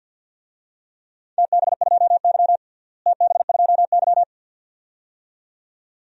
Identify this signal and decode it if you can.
Morse code, heard as a keyed tone, about 52 wpm, 700 Hz. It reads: TB2Y TB2Y
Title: DXCC-100 Call Signs - 4 Characters - Repeated 2x - 50wpm